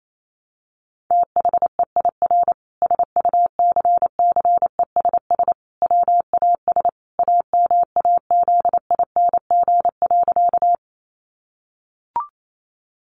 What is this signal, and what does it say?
Morse code, heard as a keyed tone, about 28 wpm, 700 Hz. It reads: THEIR SUCCESS WAS AMAZING.
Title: Sentences from Top 1500 Words 28wpm